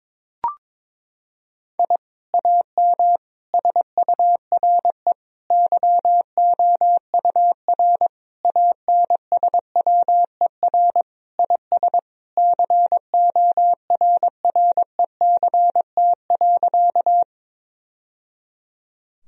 Morse code, heard as a keyed tone, about 22 wpm, 700 Hz. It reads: I AM SURE YOUR ANSWER IS CORRECT.
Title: Sentences from Top 1000 Words - Review Part 2 - 22wpm